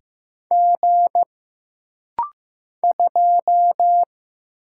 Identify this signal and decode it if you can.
Morse code, heard as a keyed tone, about 15 wpm, 700 Hz. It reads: G 2
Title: Single Letter-Number Alt - Rapid-Fire 15wpm